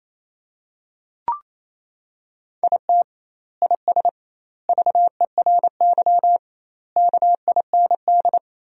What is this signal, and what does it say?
Morse code, heard as a keyed tone, about 28 wpm, 700 Hz. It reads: IT IS VERY KIND
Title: 2x Word-Spacing - Sentences from Top 200 Words - 28wpm